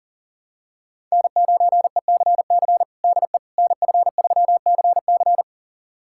Morse code, heard as a keyed tone, about 37 wpm, 700 Hz. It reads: N9ECC DE NF3CC
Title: USA Callsign DE Callsign 40wpm